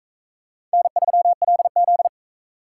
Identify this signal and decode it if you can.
Morse code, heard as a keyed tone, about 38 wpm, 700 Hz. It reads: N3LZ